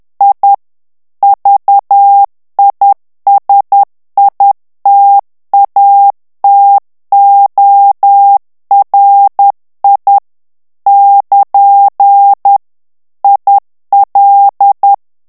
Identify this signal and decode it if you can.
Morse code, heard as a keyed tone, about 10 wpm, 800 Hz. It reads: I VISITATORI <KN> IL
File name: morse mp3 / ITALIEN texte clair 1 vitesse 10 mots minute